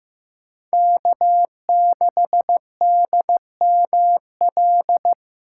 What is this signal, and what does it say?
Morse code, heard as a keyed tone, about 15 wpm, 700 Hz. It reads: K6DML
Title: USA Call Signs - 6x Character Spacing - 15wpm